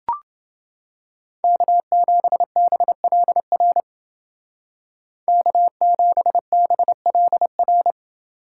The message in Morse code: K7BLR K7BLR

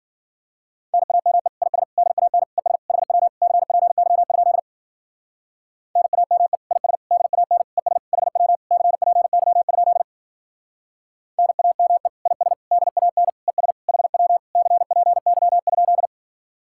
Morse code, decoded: NAME IS DAN ES HW CPY? NAME IS DAN ES HW CPY? NAME IS DAN ES HW CPY?

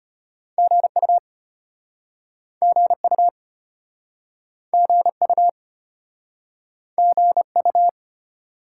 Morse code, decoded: GU GU GU GU

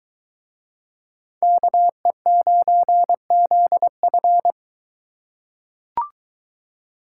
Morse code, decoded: KE9ZF

